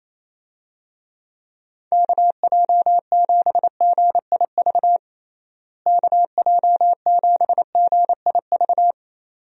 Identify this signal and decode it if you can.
Morse code, heard as a keyed tone, about 28 wpm, 700 Hz. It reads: KJ7GIV KJ7GIV
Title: USA Call Signs - 6 Characters - Repeated 2x - 28wpm